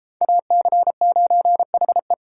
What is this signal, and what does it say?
Morse code, heard as a keyed tone, about 31 wpm, 700 Hz. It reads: AC9HE